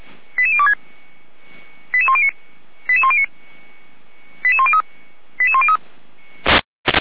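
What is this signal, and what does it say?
Five-tone selective call (ZVEI), heard as quick runs of five tones, tones of about 70 ms. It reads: 88026 88109 88109 88103 88103
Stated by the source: ZVEI five-tone selective call 88102, received on SDR, from audio.f85360.feuerwehr-orlamuende.org fire brigade server.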